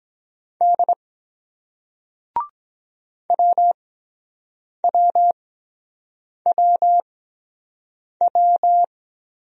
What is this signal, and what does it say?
Morse code, heard as a keyed tone, about 22 wpm, spaced as for 7 wpm, 700 Hz. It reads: D WWWW